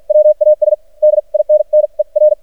Morse code, heard as a keyed tone, about 49 wpm, spaced as for 40 wpm, 610 Hz. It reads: WAS DINNER
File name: archived / CRSnet